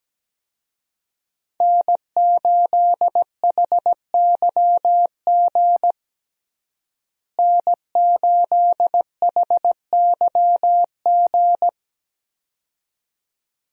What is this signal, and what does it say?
Morse code, heard as a keyed tone, about 17 wpm, 700 Hz. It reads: N8HYG N8HYG
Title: USA Call Signs - 5 Characters - Repeated 2x - 17wpm